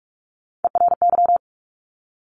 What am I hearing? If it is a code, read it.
Morse code, heard as a keyed tone, about 46 wpm, 700 Hz. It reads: ERY
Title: Top Three Letter Combinations in Words 45wpm